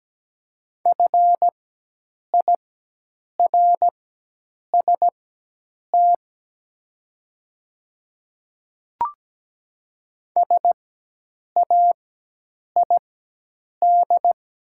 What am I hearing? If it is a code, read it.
Morse code, heard as a keyed tone, about 17 wpm, spaced as for 8 wpm, 700 Hz. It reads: FIRST SAID
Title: 4x Character Spacing - Top 100 Words - 17wpm